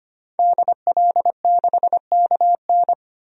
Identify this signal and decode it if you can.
Morse code, heard as a keyed tone, about 25 wpm, 700 Hz. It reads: DL6KN
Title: DXCC-100 Call Signs - 5 Characters - 25wpm